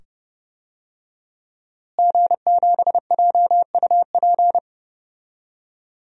Morse code, heard as a keyed tone, about 30 wpm, 700 Hz. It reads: G7JUP
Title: DXCC-100 Call Signs - 5 Characters - Repeated 2x - 30wpm